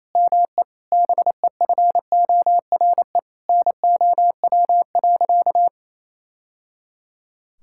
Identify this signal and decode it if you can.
Morse code, heard as a keyed tone, about 27 wpm, 700 Hz. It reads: ME BEFORE NOW.
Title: Sentences from Top 1000 Words - Review Part 6 - 28wpm